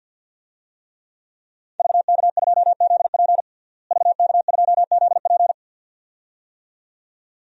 Morse code, heard as a keyed tone, about 51 wpm, 700 Hz. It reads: VK2ZP VK2ZP
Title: DXCC-100 Call Signs - 5 Characters - Repeated 2x - 50wpm